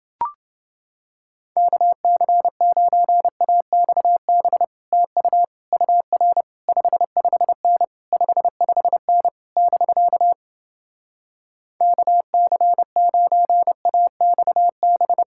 Morse code, heard as a keyed tone, about 31 wpm, 700 Hz. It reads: KC9AXB TU UR 55N 55N <BK> KC9AXB